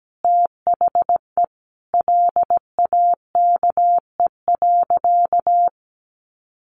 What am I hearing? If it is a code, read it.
Morse code, heard as a keyed tone, about 17 wpm, 700 Hz. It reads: THE LAKE.
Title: Sentences from Top 1500 Words 17wpm